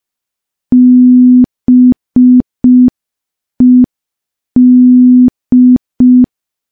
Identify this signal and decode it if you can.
Morse code, heard as a keyed tone, about 5 wpm, 250 Hz. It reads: BED
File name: morse / morse-training-05WPM